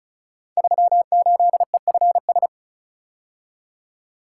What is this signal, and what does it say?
Morse code, heard as a keyed tone, about 36 wpm, 690 Hz. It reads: 38EFS
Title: Code Groups - 5 Letter:Numbers - 35wpm